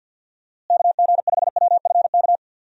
Morse code, heard as a keyed tone, about 48 wpm, 700 Hz. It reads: KG5WFK
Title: USA Call Signs - 6 Characters - Repeated 2x - 50wpm